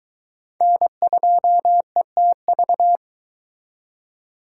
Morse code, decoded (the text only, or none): N2ETV